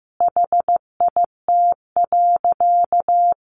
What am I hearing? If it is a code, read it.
Morse code, heard as a keyed tone, about 15 wpm, 700 Hz. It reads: HIT.
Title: Sentences from Top 1200 Words 15wpm